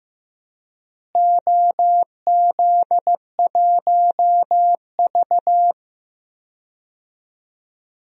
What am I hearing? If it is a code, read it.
Morse code, heard as a keyed tone, about 15 wpm, 700 Hz. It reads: OZ1V
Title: DXCC-100 Call Signs - 4 Characters - Repeated 2x - 15wpm